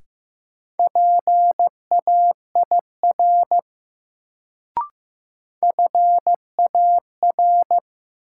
Morse code, heard as a keyed tone, about 15 wpm, 700 Hz. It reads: PAIR FAR
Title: Sets of 3 Words - New Word Drill - 15wpm